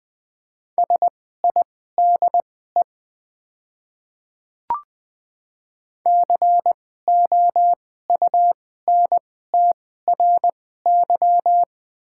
Morse code, heard as a keyed tone, about 20 wpm, spaced as for 14 wpm, 700 Hz. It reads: SIDE COUNTRY
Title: 2x Character Spacing - Top 300 Words - 20wpm